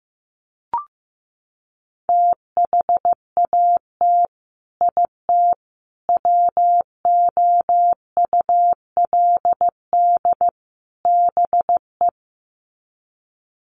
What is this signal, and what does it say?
Morse code, heard as a keyed tone, about 15 wpm, 700 Hz. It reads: THAT IT WOULD BE
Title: Sets of 4 Words 15wpm